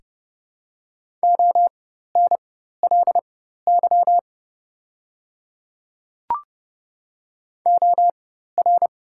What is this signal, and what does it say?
Morse code, heard as a keyed tone, about 32 wpm, spaced as for 14 wpm, 700 Hz. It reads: ONLY OR
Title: 4x Character Spacing - Top 300 Words - 30wpm